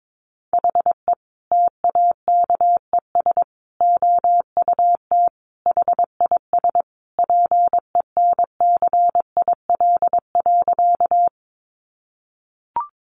Morse code, heard as a keyed tone, about 22 wpm, 700 Hz. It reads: HE TAKES OUT HIS PENCIL.
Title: Additional Easy Phrases 22wpm